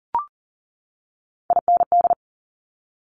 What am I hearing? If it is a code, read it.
Morse code, heard as a keyed tone, about 40 wpm, 700 Hz. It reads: IND
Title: Top Three Letter Combinations in Words 40wpm